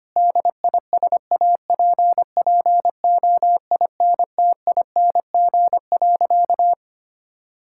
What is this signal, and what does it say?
Morse code, heard as a keyed tone, about 25 wpm, 700 Hz. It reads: DISAPPOINTING.
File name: practice / 25wpm / Sentences with 4-Word Phrases - 25wpm